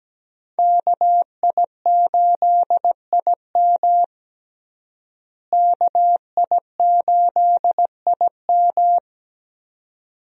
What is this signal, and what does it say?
Morse code, heard as a keyed tone, about 17 wpm, 700 Hz. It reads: KI8IM KI8IM
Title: USA Call Signs - 5 Characters - Repeated 2x - 17wpm